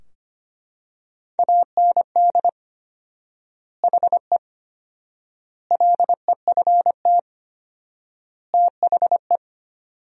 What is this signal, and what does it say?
Morse code, heard as a keyed tone, about 25 wpm, 700 Hz. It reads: AND HE LEFT THE
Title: 4x Word-Spacing - Sentences from Top 300 Words - 25wpm